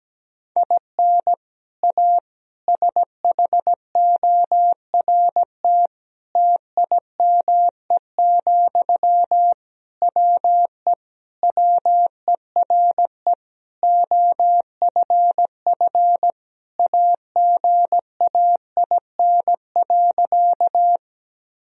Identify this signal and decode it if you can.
Morse code, heard as a keyed tone, about 17 wpm, 700 Hz. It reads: IN A SHORT TIME, WE WERE OFF AGAIN.
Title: Sentences from Top 1000 Words - Review Part 2 - 17wpm